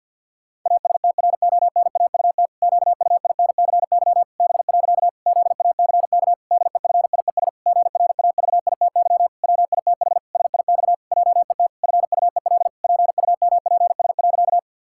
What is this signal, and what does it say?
Morse code, heard as a keyed tone, about 51 wpm, 700 Hz. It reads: ASTRONAUT QUINCY B. ZACK DEFIES GRAVITY WITH SIX JET FUEL PUMPS.
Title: Pangram Challenge v2 - 50wpm